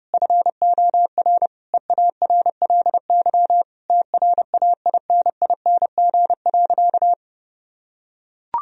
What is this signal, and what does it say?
Morse code, heard as a keyed tone, about 30 wpm, 700 Hz. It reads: FOR EARLY TRAINING.
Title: Sentences from Top 1000 Words - Review Part 4 - 30wpm